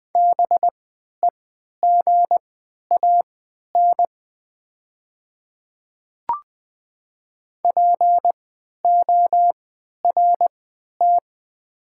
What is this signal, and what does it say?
Morse code, heard as a keyed tone, about 20 wpm, spaced as for 11 wpm, 690 Hz. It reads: BEGAN PORT
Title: 3x Character Spacing - Top 300 Words - 20wpm